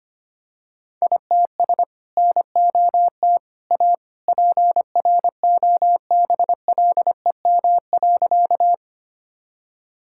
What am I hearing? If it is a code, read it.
Morse code, heard as a keyed tone, about 25 wpm, 700 Hz. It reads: ITS NOT A PROBLEM.
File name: practice / 25wpm / Sentences with 2-Word Phrases - 25wpm